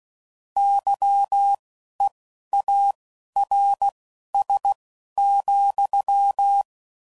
Morse code, heard as a keyed tone, about 16 wpm, spaced as for 11 wpm, 800 Hz. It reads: YEARS,
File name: CW Code Groups @10WPM/Technical Subjects